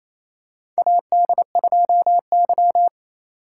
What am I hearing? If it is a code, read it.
Morse code, heard as a keyed tone, about 28 wpm, 700 Hz. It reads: AD2Y
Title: USA Call Signs - 4 Characters - 4x Character Spacing - 28wpm